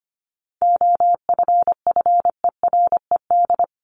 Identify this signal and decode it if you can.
Morse code, heard as a keyed tone, about 25 wpm, 700 Hz. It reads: OFFERED
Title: ED Suffix - Top 100 Words with Base Words 25wpm